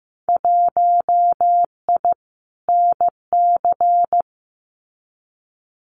Morse code, heard as a keyed tone, about 15 wpm, 700 Hz. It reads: 1I NC
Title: Winter Field Day Exchanges 15wpm